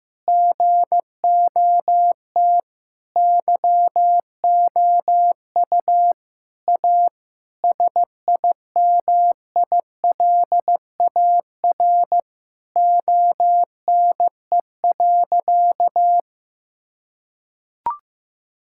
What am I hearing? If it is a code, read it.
Morse code, heard as a keyed tone, about 15 wpm, 700 Hz. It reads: GOT YOU A SIMILAR ONE.
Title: Sentences from Top 1000 Words - Review Part 5 - 15wpm